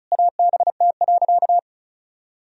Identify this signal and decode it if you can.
Morse code, heard as a keyed tone, about 36 wpm, 700 Hz. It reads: ABT.
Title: Sentences from Top 400 Words and Abbreviations 35wpm